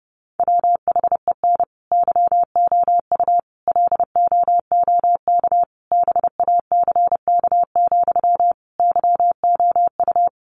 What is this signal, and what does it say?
Morse code, heard as a keyed tone, about 30 wpm, 700 Hz. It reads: WHEN YOU LOOK BACK, YOU
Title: Sentences from Top 800 Words 30wpm